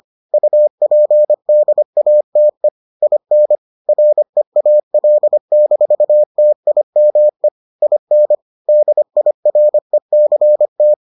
Morse code, heard as a keyed tone, about 25 wpm, 600 Hz. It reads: UPDATE IN REAL-TIME IN DIRECT